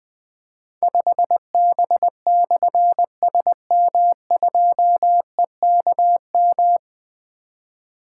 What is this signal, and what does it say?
Morse code, heard as a keyed tone, about 20 wpm, 700 Hz. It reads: 5B/SM2EKM
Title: Difficult Call Signs 20wpm